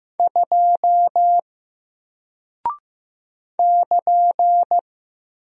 Morse code, heard as a keyed tone, about 15 wpm, 700 Hz. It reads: 2 <KN>
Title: Single Character (Letter:Number:Punctuation:Prosign) Alt 15wpm